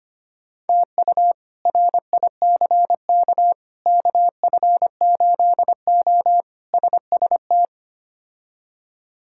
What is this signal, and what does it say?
Morse code, heard as a keyed tone, about 25 wpm, 700 Hz. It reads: TU RICK KF8O SST
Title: K1USN SST QSO Practice - 25wpm